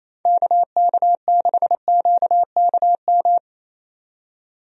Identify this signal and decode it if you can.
Morse code, heard as a keyed tone, about 27 wpm, 700 Hz. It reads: KK6QKM
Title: USA Call Signs - 6 Characters - 28wpm